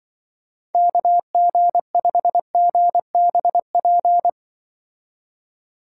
Morse code, heard as a keyed tone, about 23 wpm, 700 Hz. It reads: KG5GBP
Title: Speed Racing - USA 2x3 Call Signs 20wpm